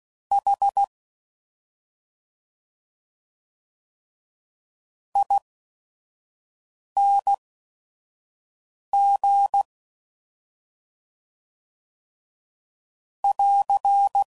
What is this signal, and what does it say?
Morse code, heard as a keyed tone, about 16 wpm, spaced as for 5 wpm, 800 Hz. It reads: H ING <AR>
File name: Suffixes / Special ING's / Sngl ING's